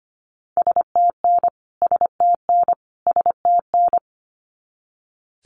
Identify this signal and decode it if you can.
Morse code, heard as a keyed tone, about 25 wpm, 700 Hz. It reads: STN STN STN
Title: Abbreviations Repeated 3x - 25wpm